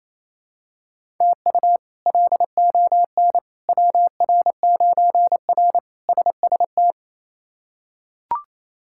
Morse code, heard as a keyed tone, about 28 wpm, 700 Hz. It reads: TU LON WR9R SST
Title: K1USN SST QSO Practice - 28wpm